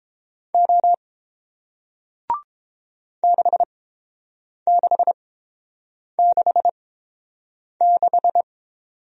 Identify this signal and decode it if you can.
Morse code, heard as a keyed tone, about 30 wpm, spaced as for 7 wpm, 700 Hz. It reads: O 6666